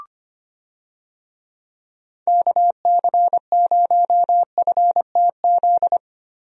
Morse code, heard as a keyed tone, about 25 wpm, 700 Hz. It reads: KC0FTZ
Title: USA Call Signs - 6 Characters - 25wpm